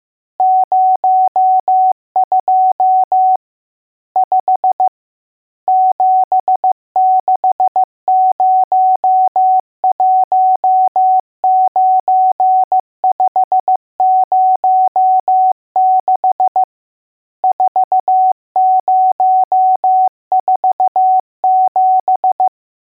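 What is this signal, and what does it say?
Morse code, heard as a keyed tone, about 15 wpm, 750 Hz. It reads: 02 5 76019506 4047
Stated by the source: Morse code lesson: test on all digits.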